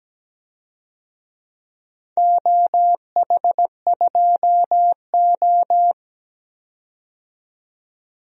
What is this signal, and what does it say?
Morse code, heard as a keyed tone, about 17 wpm, 700 Hz. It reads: OH2O
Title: DXCC-100 Call Signs - 4 Characters - 17wpm